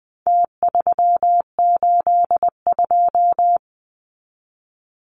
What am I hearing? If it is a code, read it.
Morse code, heard as a keyed tone, about 20 wpm, 700 Hz. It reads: T382
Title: Numbers - Contest Style - 4-Digit 20wpm